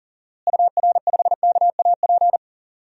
Morse code, heard as a keyed tone, about 40 wpm, 700 Hz. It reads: UR5KAP
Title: DXCC-100 Call Signs - 6 Characters - Repeated 3x - 40wpm